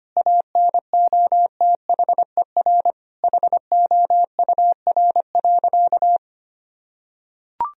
Morse code, heard as a keyed tone, about 25 wpm, 700 Hz. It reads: ANOTHER HOUR.
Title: Sentences from Top 1000 Words - Review Part 4 - 25wpm